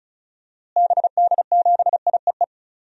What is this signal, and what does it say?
Morse code, heard as a keyed tone, about 36 wpm, 700 Hz. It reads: BD7IEE
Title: DXCC-100 Call Signs - 6 Characters - Repeated 2x - 35wpm